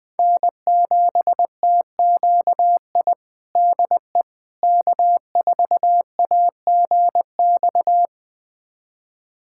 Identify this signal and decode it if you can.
Morse code, heard as a keyed tone, about 20 wpm, 700 Hz. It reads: N7TQI DE K4AGX
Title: USA Callsign DE Callsign 20wpm